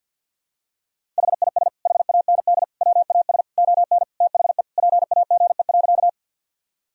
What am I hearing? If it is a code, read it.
Morse code, heard as a keyed tone, about 46 wpm, 700 Hz. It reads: HIS HAND WAS ON THE PAGE.
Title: Sentences from Top 1000 Words - Review Part 1 - 50wpm